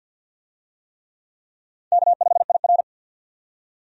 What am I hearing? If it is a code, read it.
Morse code, heard as a keyed tone, about 48 wpm, 700 Hz. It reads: K5IR
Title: USA Call Signs - 4 Characters - 50wpm